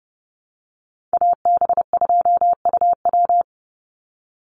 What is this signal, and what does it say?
Morse code, heard as a keyed tone, about 30 wpm, 700 Hz. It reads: AB2UW
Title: USA Call Signs 30wpm